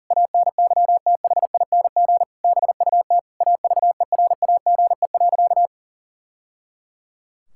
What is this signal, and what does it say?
Morse code, heard as a keyed tone, about 37 wpm, 700 Hz. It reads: ANYTHING BUT AVERAGE.
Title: Sentences from Top 1000 Words - Review Part 6 - 40wpm